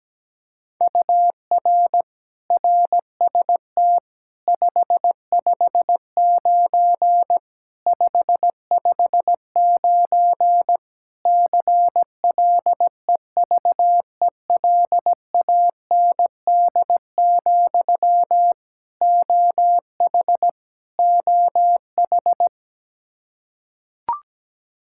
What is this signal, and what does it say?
Morse code, heard as a keyed tone, about 17 wpm, 700 Hz. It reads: UR RST 559 559 CLEVELAND, OH OH
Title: K1USN SST QSO Practice 17wpm